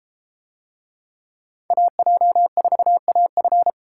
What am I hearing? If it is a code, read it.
Morse code, heard as a keyed tone, about 32 wpm, 700 Hz. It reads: AJ4AF